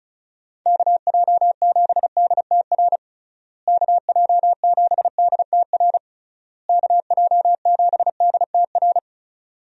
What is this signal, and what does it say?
Morse code, heard as a keyed tone, about 36 wpm, 700 Hz. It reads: KJ7DTR KJ7DTR KJ7DTR
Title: USA Call Signs - 6 Characters - Repeated 3x - 35wpm